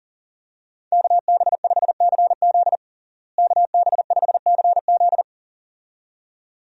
Morse code, heard as a keyed tone, about 40 wpm, 700 Hz. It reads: KB5CZ KB5CZ
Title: USA Call Signs - 5 Characters - Repeated 2x - 40wpm